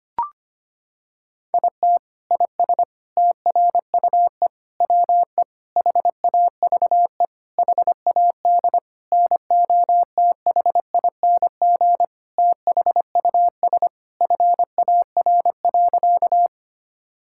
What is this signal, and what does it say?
Morse code, heard as a keyed tone, about 25 wpm, 700 Hz. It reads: IT IS TRUE WE HAVE HAD NOTHING THUS FAR.
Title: Sentences from Top 1000 Words - Review Part 5 - 25wpm